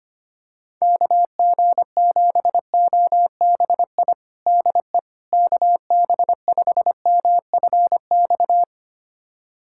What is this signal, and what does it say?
Morse code, heard as a keyed tone, about 25 wpm, 700 Hz. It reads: KG7OBI DE KB5MFX
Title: USA Callsign DE Callsign - 6 Characters - 25wpm